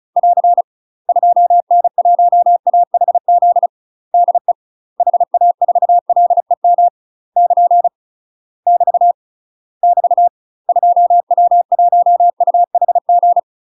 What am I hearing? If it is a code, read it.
Morse code, heard as a keyed tone, about 36 wpm, 700 Hz. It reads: <AR> 2N1AHZ DE HA4LEM <KN> <BT> <BT> 2W1UHG
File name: UK MASTER QSOs TRAINING/35WPM